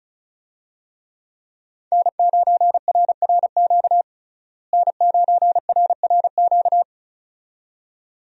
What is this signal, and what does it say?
Morse code, heard as a keyed tone, about 34 wpm, 700 Hz. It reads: N9RRQ N9RRQ
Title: USA Call Signs - 5 Characters - Repeated 2x - 35wpm